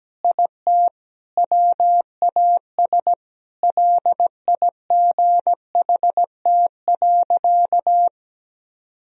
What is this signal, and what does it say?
Morse code, heard as a keyed tone, about 17 wpm, 700 Hz. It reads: IT WAS LIGHT.
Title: Sentences from Top 1000 Words - Review Part 1 - 17wpm